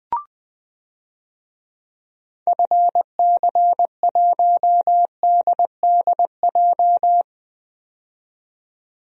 Morse code, heard as a keyed tone, about 20 wpm, 700 Hz. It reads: FC1DDJ
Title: DXCC-100 Call Signs - 6 Characters - 20wpm